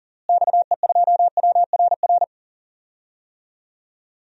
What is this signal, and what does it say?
Morse code, heard as a keyed tone, about 40 wpm, 700 Hz. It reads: XE2WRR
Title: DXCC-100 Call Signs - 6 Characters - Repeated 2x - 40wpm